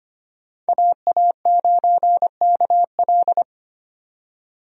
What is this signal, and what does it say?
Morse code, heard as a keyed tone, about 25 wpm, 700 Hz. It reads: AA9KL
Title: USA Call Signs - 5 Characters - 25wpm